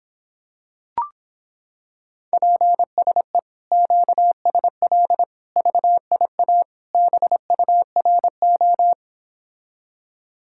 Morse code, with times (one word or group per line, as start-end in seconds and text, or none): 2.3-3.4 PSE
3.7-5.2 QSL
5.6-6.6 VIA
6.9-8.9 BURO